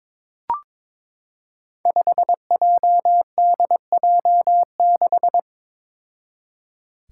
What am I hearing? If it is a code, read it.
Morse code, heard as a keyed tone, about 22 wpm, 700 Hz. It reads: 5JDJ6
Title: Five Challenging Letter-Number Combinations 22wpm